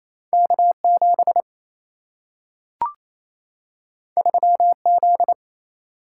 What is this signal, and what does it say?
Morse code, heard as a keyed tone, about 28 wpm, 700 Hz. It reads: K 7 3 Z
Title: Single-Letter Single-Number 28wpm